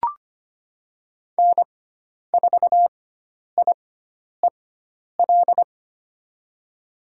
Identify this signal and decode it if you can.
Morse code, heard as a keyed tone, about 25 wpm, spaced as for 10 wpm, 700 Hz. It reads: N4IEL
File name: Lesson 16g - Letter c - Call Signs 25wpm